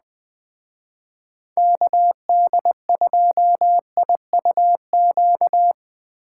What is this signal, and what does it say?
Morse code, heard as a keyed tone, about 20 wpm, 700 Hz. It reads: KD2IUQ
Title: Speed Racing - USA 2x3 Call Signs 17wpm